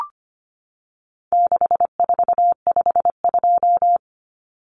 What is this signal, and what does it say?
Morse code, heard as a keyed tone, about 25 wpm, 700 Hz. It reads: 6452
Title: Common Cut Numbers - 4-Digit 25wpm